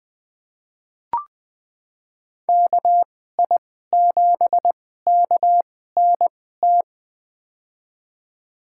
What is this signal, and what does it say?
Morse code, heard as a keyed tone, about 20 wpm, spaced as for 14 wpm, 700 Hz. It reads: KI7KNT
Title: USA Call Signs - 6 Characters - 2x Character Spacing - 20wpm